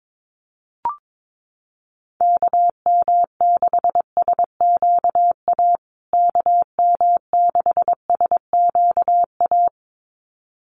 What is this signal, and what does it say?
Morse code, heard as a keyed tone, about 22 wpm, 700 Hz. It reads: KM6SQA KM6SQA
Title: USA Call Signs Repeated 2x 22wpm